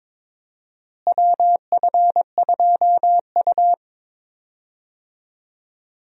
Morse code, heard as a keyed tone, about 21 wpm, 700 Hz. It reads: WF2U
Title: USA Call Signs - 4 Characters - Repeated 3x - 22wpm